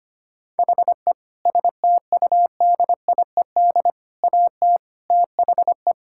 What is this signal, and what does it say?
Morse code, heard as a keyed tone, about 25 wpm, 700 Hz. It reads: HE STUDIED AT THE
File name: practice / 25wpm / Sentences with 4-Word Phrases - 25wpm